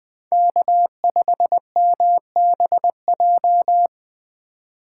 Morse code, heard as a keyed tone, about 20 wpm, 700 Hz. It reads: K5MBJ
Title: USA Call Signs - 5 Characters - 3x Character Spacing - 20wpm